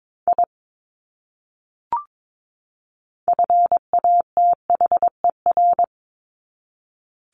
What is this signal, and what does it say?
Morse code, heard as a keyed tone, about 22 wpm, 700 Hz. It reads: I FATHER
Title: Top 200 Words 22wpm